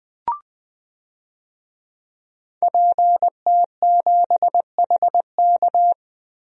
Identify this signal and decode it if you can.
Morse code, heard as a keyed tone, about 20 wpm, 700 Hz. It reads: PT7HK
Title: DXCC-100 Call Signs - 5 Characters - 20wpm